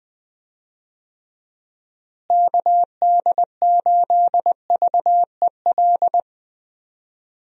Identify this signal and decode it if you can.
Morse code, heard as a keyed tone, about 20 wpm, 700 Hz. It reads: KD8VEL
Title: USA Call Signs - 6 Characters - Repeated 3x - 20wpm